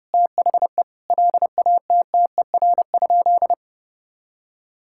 Morse code, heard as a keyed tone, about 32 wpm, 700 Hz. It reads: THE LATTER?
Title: Sentences from Top 1600 Words 30wpm